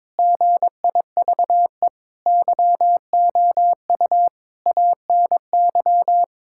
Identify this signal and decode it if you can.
Morse code, heard as a keyed tone, about 22 wpm, 700 Hz. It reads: GIVE YOU ANY
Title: Sentences from Top 1600 Words 22wpm